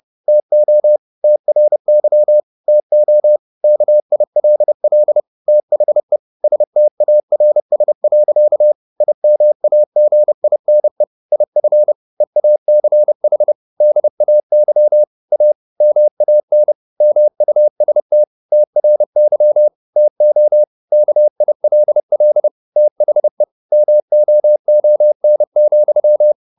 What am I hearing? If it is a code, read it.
Morse code, heard as a keyed tone, about 30 wpm, 600 Hz. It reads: TO TRY TO KILL THE STARS. IMAGINE IF EACH DAY A MAN MUST TRY TO KILL THE MOON,